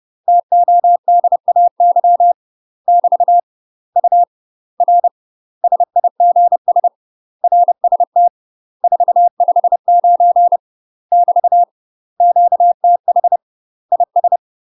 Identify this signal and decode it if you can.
Morse code, heard as a keyed tone, about 30 wpm, 700 Hz. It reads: TODAY <BT> U R SIGS RST 459 <BT> QTH IS
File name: UK MASTER QSOs TRAINING / 30WPM / UK MASTER QSOs 0015